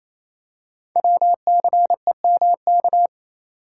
Morse code, heard as a keyed tone, about 29 wpm, 700 Hz. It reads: WCEMK
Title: Code Groups - 5 Letters - 28wpm